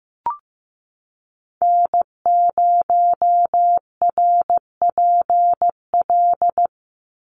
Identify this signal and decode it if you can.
Morse code, heard as a keyed tone, about 15 wpm, 700 Hz. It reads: N0RPL
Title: USA Call Signs 15wpm